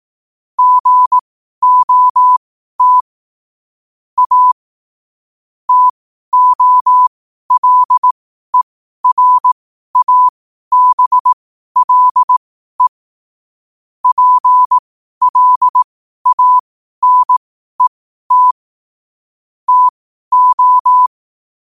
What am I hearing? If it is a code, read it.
Morse code, heard as a keyed tone, about 18 wpm, spaced as for 12 wpm, 1000 Hz. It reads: GOT A TOLERABLE PLANET TO
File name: qotdfiles / 12wpm